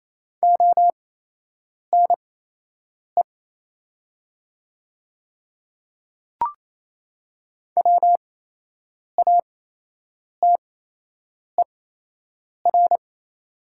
Morse code, heard as a keyed tone, about 28 wpm, spaced as for 8 wpm, 700 Hz. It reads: ONE WATER
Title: 8x Character Spacing - Top 100 Words - 28wpm